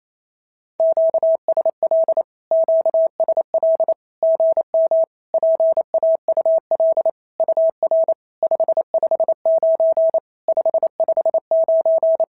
Morse code, heard as a keyed tone, about 28 wpm, 650 Hz. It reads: QSL QSL GM PAUL UR 559 559
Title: SOTA QSO Practice 28wpm